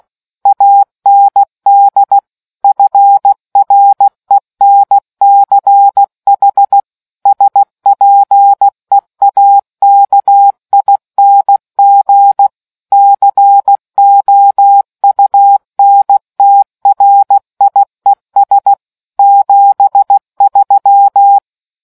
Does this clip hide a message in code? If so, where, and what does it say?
Yes: Morse code from 0.5 to 21.4 s, heard as a keyed tone, about 16 wpm, 800 Hz. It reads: AND FRENCH SPEAKING COUNTRIES 73